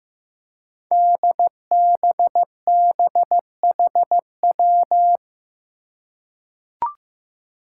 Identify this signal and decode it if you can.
Morse code, heard as a keyed tone, about 15 wpm, 700 Hz. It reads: DBBHW
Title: Five Challenging Letter-Number Combinations 15wpm